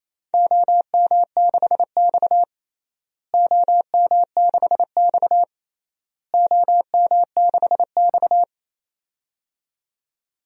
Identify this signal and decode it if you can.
Morse code, heard as a keyed tone, about 28 wpm, 700 Hz. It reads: OM6X OM6X OM6X
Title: DXCC-100 Call Signs - 4 Characters - Repeated 3x - 28wpm